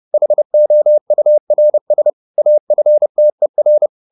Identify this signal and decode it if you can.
Morse code, heard as a keyed tone, about 30 wpm, 600 Hz. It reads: HOURS AFTER
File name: the old man and the sea / the old man and the sea 1 30wpm